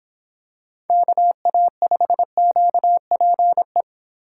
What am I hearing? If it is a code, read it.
Morse code, heard as a keyed tone, about 24 wpm, 700 Hz. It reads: KA5QPE